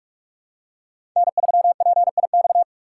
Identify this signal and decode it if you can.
Morse code, heard as a keyed tone, about 46 wpm, 700 Hz. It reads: N3PIX
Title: USA Call Signs - 5 Characters - 3x Character Spacing - 45wpm